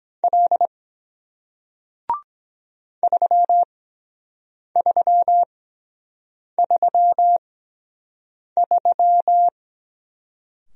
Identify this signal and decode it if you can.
Morse code, heard as a keyed tone, about 23 wpm, spaced as for 7 wpm, 700 Hz. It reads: L 3333